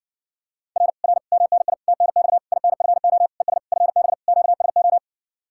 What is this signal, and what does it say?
Morse code, heard as a keyed tone, about 51 wpm, 700 Hz. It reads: R R MNI TNX INFO ES FB QSO